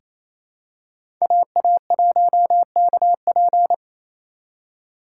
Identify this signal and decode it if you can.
Morse code, heard as a keyed tone, about 28 wpm, 700 Hz. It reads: AA1KP
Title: USA Call Signs - 5 Characters - 2x Character Spacing - 28wpm